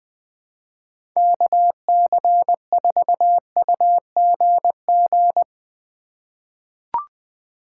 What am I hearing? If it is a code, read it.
Morse code, heard as a keyed tone, about 21 wpm, 700 Hz. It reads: KC4UGG
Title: USA Call Signs - 6 Characters - 4x Character Spacing - 20wpm